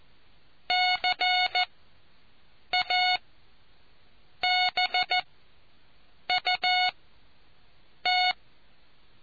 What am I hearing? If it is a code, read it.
Morse code, heard as a keyed tone, about 14 wpm, spaced as for 6 wpm, 730 Hz. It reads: CABUT